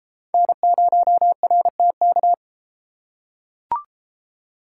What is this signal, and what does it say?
Morse code, heard as a keyed tone, about 32 wpm, 700 Hz. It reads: N0RTK